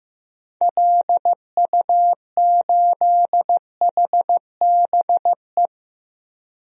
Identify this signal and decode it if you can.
Morse code, heard as a keyed tone, about 15 wpm, 700 Hz. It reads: LU8HBE
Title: DXCC-100 Call Signs - 6 Characters - Repeated 3x - 15wpm